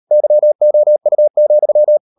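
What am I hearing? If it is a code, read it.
Morse code, heard as a keyed tone, about 38 wpm, 600 Hz. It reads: YOU,
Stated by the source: Morse code practice at 38 WPM, sending Ken Follett's novel Never in English, from CW med Gnister.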